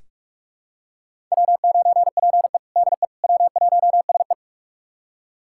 Morse code, heard as a keyed tone, about 44 wpm, 690 Hz. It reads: W9PE DE W1SE
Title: USA Callsign DE Callsign - 4 Characters - 45wpm